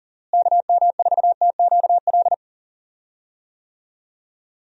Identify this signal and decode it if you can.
Morse code, heard as a keyed tone, about 37 wpm, 700 Hz. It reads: KM4TQL
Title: USA Call Signs - 6 Characters - Repeated 3x - 40wpm